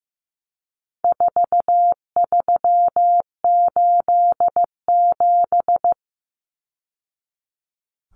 Morse code, heard as a keyed tone, about 15 wpm, 700 Hz. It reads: 4387
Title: Numbers - 4-Digit 15wpm